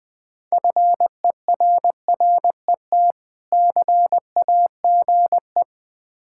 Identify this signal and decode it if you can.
Morse code, heard as a keyed tone, about 20 wpm, 700 Hz. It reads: FERRET CAGE
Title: Difficult Words with Hints - 20wpm